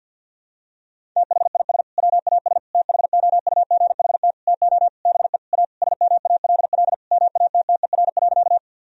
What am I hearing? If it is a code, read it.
Morse code, heard as a keyed tone, about 50 wpm, 700 Hz. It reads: THIS WAS THOUGHT TO BE A SMALL MATTER.
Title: Sentences from Top 1000 Words - Review Part 1 - 50wpm